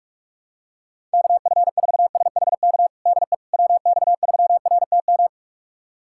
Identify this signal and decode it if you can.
Morse code, heard as a keyed tone, about 46 wpm, 700 Hz. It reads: KF4SHK DE WX3RTM
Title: USA Callsign DE Callsign - 6 Characters - 45wpm